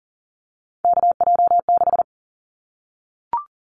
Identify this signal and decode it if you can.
Morse code, heard as a keyed tone, about 40 wpm, 700 Hz. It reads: KJ6
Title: Callsign Prefixes for Top 25 DXCC - 40wpm